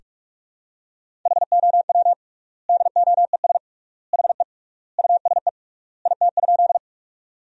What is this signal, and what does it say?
Morse code, heard as a keyed tone, about 41 wpm, 700 Hz. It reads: HOW DOES HE USE IT?
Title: 3x Word-Spacing - Sentences from Top 200 Words - 45wpm